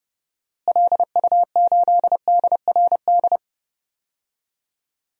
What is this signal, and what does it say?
Morse code, heard as a keyed tone, about 30 wpm, 700 Hz. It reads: LU8DRD
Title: DXCC-100 Call Signs - 6 Characters - Repeated 3x - 30wpm